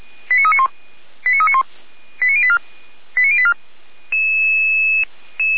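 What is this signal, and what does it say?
Five-tone selective call (ZVEI), heard as quick runs of five tones, tones of about 70 ms. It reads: 89391 89391 89084 89084
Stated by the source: Five-tone ZVEI selective call 89503 received on an SDR.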